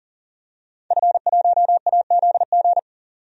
Morse code, heard as a keyed tone, about 40 wpm, 700 Hz. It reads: F1AZG
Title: DXCC-100 Call Signs - 5 Characters - 40wpm